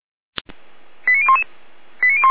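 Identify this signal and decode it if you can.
Five-tone selective call (ZVEI), heard as quick runs of five tones, tones of about 70 ms. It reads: 89011
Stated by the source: ZVEI five-tone call 89303 received on SDR.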